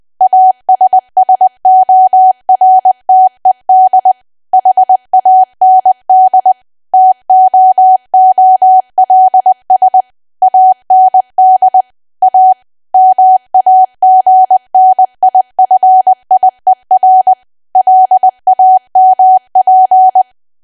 Morse code, heard as a keyed tone, about 20 wpm, 750 Hz. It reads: ASSORTED HAND TOOLS AND A MAGNIFIER LAMP
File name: cwa / W1AW-Level3 / 20WPM